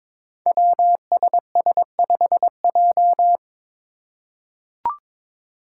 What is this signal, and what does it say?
Morse code, heard as a keyed tone, about 22 wpm, 700 Hz. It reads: WSS5J
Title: Five Challenging Letter-Number Combinations 22wpm